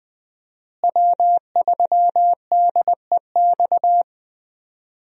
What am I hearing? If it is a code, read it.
Morse code, heard as a keyed tone, about 20 wpm, 700 Hz. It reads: W3DEX
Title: USA Call Signs - 5 Characters - Repeated 2x - 20wpm